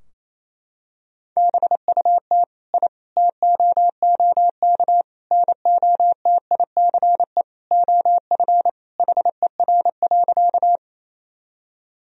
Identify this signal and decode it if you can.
Morse code, heard as a keyed tone, about 28 wpm, 700 Hz. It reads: BUT I TOOK NOTICE OF HER.
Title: Sentences from Top 1000 Words - Review Part 2 - 28wpm